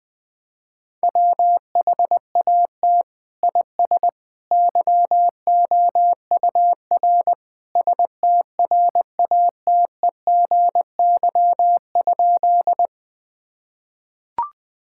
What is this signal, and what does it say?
Morse code, heard as a keyed tone, about 20 wpm, 700 Hz. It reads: WHAT IS YOUR STRATEGY?
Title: Sentences from Top 1000 Words - Review Part 5 - 20wpm